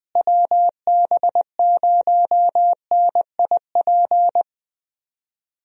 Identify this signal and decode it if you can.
Morse code, heard as a keyed tone, about 21 wpm, 700 Hz. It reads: WB0NIP
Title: USA Call Signs - 6 Characters - Repeated 2x - 20wpm